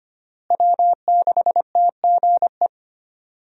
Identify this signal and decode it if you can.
Morse code, heard as a keyed tone, about 25 wpm, 700 Hz. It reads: W6TGE
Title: USA Call Signs - 5 Characters - Repeated 3x - 25wpm